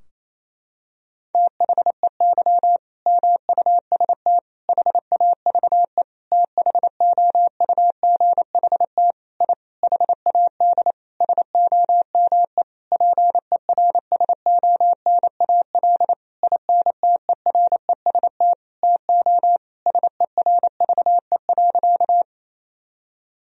Morse code, heard as a keyed tone, about 28 wpm, 700 Hz. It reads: THEY MUST HAVE THOUGHT I HAD SOME PERSONAL INTEREST TO SERVE.
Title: Sentences from Top 1000 Words - Review Part 4 - 28wpm